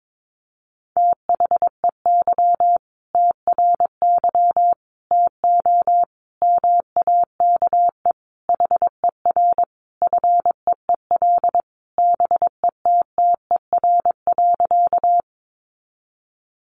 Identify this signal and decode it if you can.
Morse code, heard as a keyed tone, about 22 wpm, 700 Hz. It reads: THEY TRY TO MAKE HER FEEL BETTER.
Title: Easy Phrases 22wpm